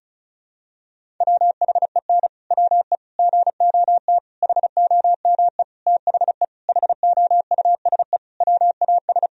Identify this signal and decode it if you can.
Morse code, heard as a keyed tone, about 36 wpm, 700 Hz. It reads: WHEN WE GOT HOME THE HOUSE WAS
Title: Sentences from Top 500 Words and Abbreviations 35wpm